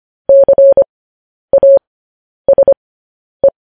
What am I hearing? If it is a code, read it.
Morse code, heard as a keyed tone, about 25 wpm, spaced as for 10 wpm, 550 Hz. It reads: CASE